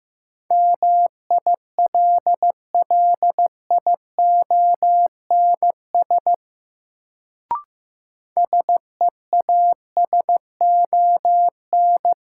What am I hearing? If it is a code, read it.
Morse code, heard as a keyed tone, about 15 wpm, 700 Hz. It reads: MILLIONS SEASON
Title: Sets of 5 Words - New Word Drill - 15wpm